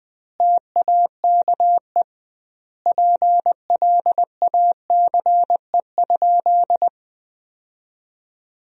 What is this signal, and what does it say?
Morse code, heard as a keyed tone, about 21 wpm, 700 Hz. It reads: TAKE PLACE?
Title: 2x Word-Spacing - Sentences from Top 200 Words - 20wpm